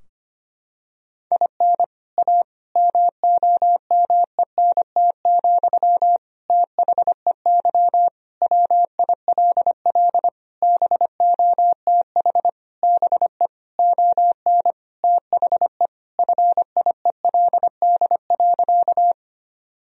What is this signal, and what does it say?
Morse code, heard as a keyed tone, about 25 wpm, 700 Hz. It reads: IN A MOMENT, THEY WILL BOTH BE ON THE FIELD.
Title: Sentences from Top 1000 Words - Review Part 2 - 25wpm